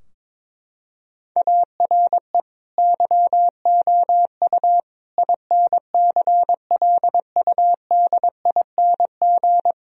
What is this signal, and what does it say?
Morse code, heard as a keyed tone, about 22 wpm, 700 Hz. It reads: ARE YOU INCLUDING
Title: Sentences from Top 1000 Words - Review Part 4 - 22wpm